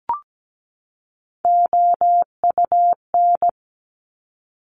Morse code, heard as a keyed tone, about 17 wpm, 700 Hz. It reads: OUN
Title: Top Three Letter Combinations in Words 17wpm